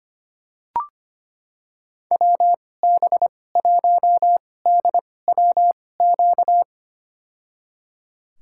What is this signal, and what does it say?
Morse code, heard as a keyed tone, about 25 wpm, spaced as for 18 wpm, 700 Hz. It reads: WB1DWQ
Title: USA Call Signs - 6 Characters - 2x Character Spacing - 25wpm